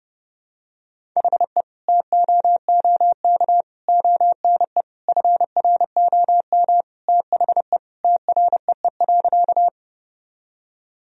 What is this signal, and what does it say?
Morse code, heard as a keyed tone, about 32 wpm, 700 Hz. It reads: HE TOOK ONE FROM THE TREE.
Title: Sentences from Top 1000 Words - Review Part 1 - 30wpm